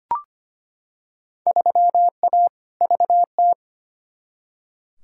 Morse code, heard as a keyed tone, about 25 wpm, 700 Hz. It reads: 3A VT
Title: Field Day Exchanges Encore - 25wpm